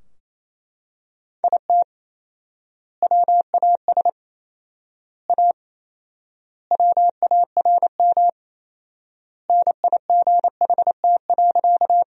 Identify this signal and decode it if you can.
Morse code, heard as a keyed tone, about 27 wpm, 700 Hz. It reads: IT WAS A WARM NIGHT.
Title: 4x Word-Spacing - Sentences from Top 500 Words - 28wpm